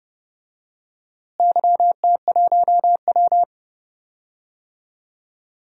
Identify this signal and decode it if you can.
Morse code, heard as a keyed tone, about 30 wpm, 700 Hz. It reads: YT1W
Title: DXCC-100 Call Signs - 4 Characters - Repeated 2x - 30wpm